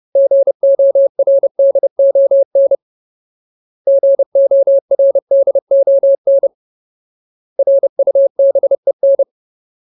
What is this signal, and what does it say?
Morse code, heard as a keyed tone, about 30 wpm, 550 Hz. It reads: GORDON GORDON RUBEN
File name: Ultimate Word List / 30WPM Ultimate